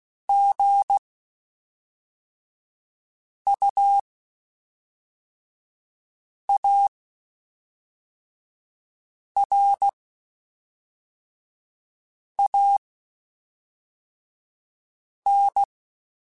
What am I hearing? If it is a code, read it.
Morse code, heard as a keyed tone, about 16 wpm, spaced as for 3 wpm, 800 Hz. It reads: GUARAN
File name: Quotes Spaced / Quotes Spaced-mp3s / Airtight sp @5